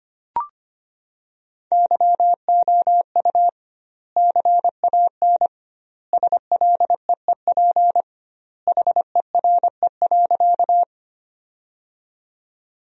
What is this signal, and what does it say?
Morse code, heard as a keyed tone, about 25 wpm, 700 Hz. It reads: YOU CAN SLEEP HERE.
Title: 2x Word-Spacing - Sentences from Top 500 Words - 25wpm